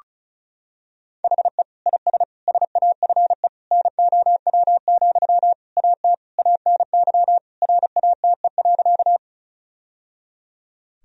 Morse code, heard as a keyed tone, about 36 wpm, 700 Hz. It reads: HE IS SAFE NOW, AT ANY RATE.
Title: Sentences from Top 1000 Words - Review Part 4 - 35wpm